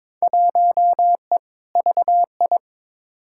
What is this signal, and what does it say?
Morse code, heard as a keyed tone, about 22 wpm, 700 Hz. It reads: 1E VI
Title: Field Day Exchanges - 22wpm